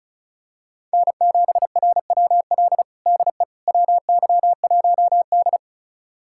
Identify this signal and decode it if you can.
Morse code, heard as a keyed tone, about 34 wpm, 700 Hz. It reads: N7RWL DE WY1D
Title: USA Callsign DE Callsign 35wpm